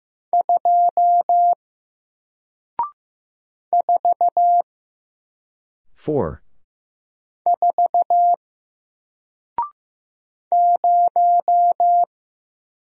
Morse code, heard as a keyed tone, about 15 wpm, 700 Hz. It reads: 2 4 4 0
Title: Single Letter-Number Alt 15wpm